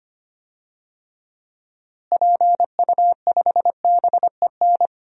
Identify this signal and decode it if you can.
Morse code, heard as a keyed tone, about 25 wpm, 700 Hz. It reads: PU5BEN
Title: DXCC-100 Call Signs - 6 Characters - Repeated 3x - 25wpm